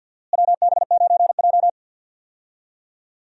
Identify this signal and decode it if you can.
Morse code, heard as a keyed tone, about 53 wpm, 700 Hz. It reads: WB9J
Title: USA Call Signs - 4 Characters - Repeated 3x - 50wpm